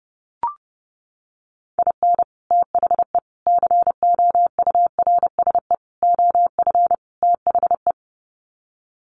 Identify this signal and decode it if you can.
Morse code, heard as a keyed tone, about 30 wpm, 700 Hz. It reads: IN THE COURSE OF THE
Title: Sets of 5 Words 30wpm